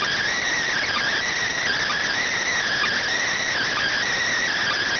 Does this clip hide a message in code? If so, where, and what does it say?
Yes: Emergency Alert System (SAME) tones from 0.7 to 4.0 s, heard as harsh buzzing bursts.